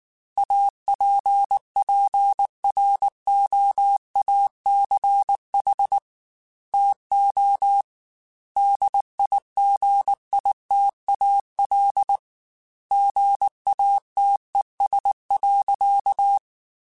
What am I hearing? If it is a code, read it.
Morse code, heard as a keyed tone, about 19 wpm, 800 Hz. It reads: APPROACH TO DIGITAL GATES.
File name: Technical Subjects / Technical Subjects-mp3s / Gates Intro @18